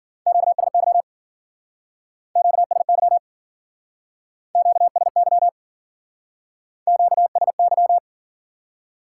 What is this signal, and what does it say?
Morse code, heard as a keyed tone, about 52 wpm, 700 Hz. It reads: QSY QSY QSY QSY